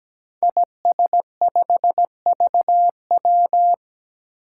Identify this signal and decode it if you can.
Morse code, heard as a keyed tone, about 17 wpm, 700 Hz. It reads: IS5VW
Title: Five Challenging Letter-Number Combinations 17wpm